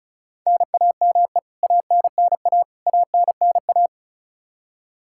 Morse code, heard as a keyed tone, about 34 wpm, 700 Hz. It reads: NAME ANNA ANNA
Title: K1USN SST QSO Practice 35wpm